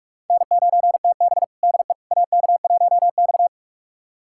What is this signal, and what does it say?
Morse code, heard as a keyed tone, about 46 wpm, 700 Hz. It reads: N9TB DE AK1X
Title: USA Callsign DE Callsign - 4 Characters - 45wpm